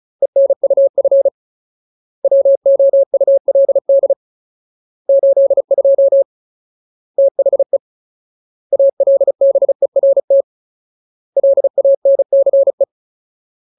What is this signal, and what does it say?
Morse code, heard as a keyed tone, about 34 wpm, 550 Hz. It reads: ENUF WOULD 82 THE ALBERT LANCE
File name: Ultimate Word List / 35WPM Ultimate